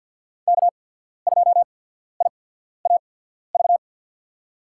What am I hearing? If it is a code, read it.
Morse code, heard as a keyed tone, about 52 wpm, spaced as for 14 wpm, 700 Hz. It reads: K2IAV
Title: USA Call Signs - 5 Characters - 8x Character Spacing - 50wpm